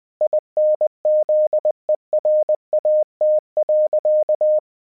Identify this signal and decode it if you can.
Morse code, heard as a keyed tone, about 20 wpm, 610 Hz. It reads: INZERAT.
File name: 20wpm